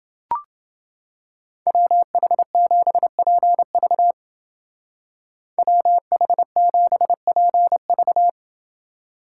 Morse code, heard as a keyed tone, about 28 wpm, 700 Hz. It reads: WH7PV WH7PV